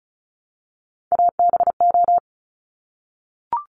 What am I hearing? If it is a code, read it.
Morse code, heard as a keyed tone, about 36 wpm, 700 Hz. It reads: ABO
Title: Top Three Letter Combinations in Words 35wpm